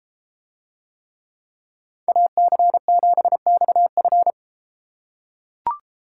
Morse code, heard as a keyed tone, about 33 wpm, 700 Hz. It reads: AC7XF